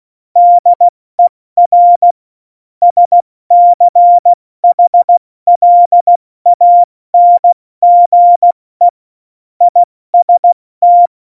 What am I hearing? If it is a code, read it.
Morse code, heard as a keyed tone, about 16 wpm, 700 Hz. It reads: DER SCHLANGE IST
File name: Morsekurs / Witze / Witz 13. November 2018, 70 F80